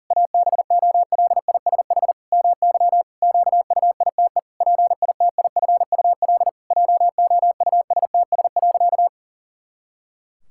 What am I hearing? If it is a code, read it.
Morse code, heard as a keyed tone, about 40 wpm, 700 Hz. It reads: ABOLISH MY QUITE PITIFUL JOUSTS.
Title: Pangram Challenge v2 - 40wpm